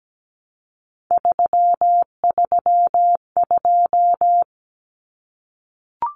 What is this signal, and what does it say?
Morse code, heard as a keyed tone, about 17 wpm, 700 Hz. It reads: 332
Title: Common Cut Numbers - 3-Digit 17wpm